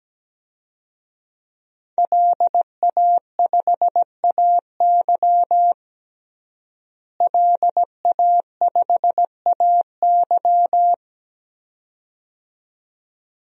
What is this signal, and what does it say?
Morse code, heard as a keyed tone, about 17 wpm, 700 Hz. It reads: LA5AY LA5AY
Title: DXCC-100 Call Signs - 5 Characters - Repeated 2x - 17wpm